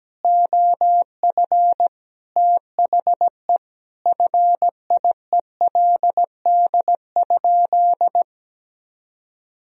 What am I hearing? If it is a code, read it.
Morse code, heard as a keyed tone, about 17 wpm, 700 Hz. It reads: OF THE FIELD?
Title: Sentences from Top 1000 Words - Review Part 2 - 17wpm